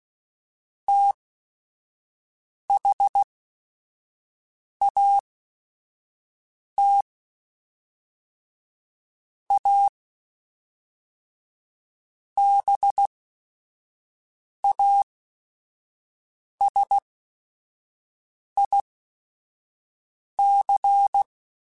Morse code, heard as a keyed tone, about 16 wpm, spaced as for 5 wpm, 800 Hz. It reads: THAT A BASIC
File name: Technical Subjects / Technical Subjects-mp3s / Sw Pwr Sup @5